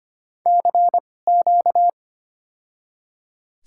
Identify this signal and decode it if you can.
Morse code, heard as a keyed tone, about 25 wpm, spaced as for 18 wpm, 700 Hz. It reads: CQ
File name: Top 90 QSO Elements - 2x Character Spacing - 25wpm